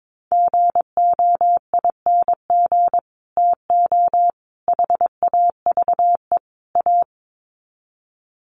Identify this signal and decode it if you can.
Morse code, heard as a keyed tone, about 22 wpm, 700 Hz. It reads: GOING TO HAVE A
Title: Sets of 4 Words 22wpm